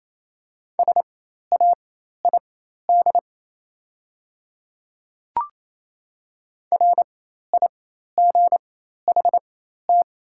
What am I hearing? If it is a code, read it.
Morse code, heard as a keyed tone, about 28 wpm, spaced as for 13 wpm, 700 Hz. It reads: SAID RIGHT
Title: 4x Character Spacing - Top 200 Words - 28wpm